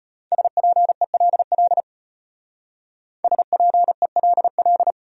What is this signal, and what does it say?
Morse code, heard as a keyed tone, about 32 wpm, 700 Hz. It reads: SPELL SPELL